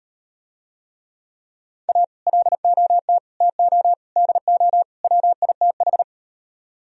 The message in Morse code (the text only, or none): A LOT TO DO WITH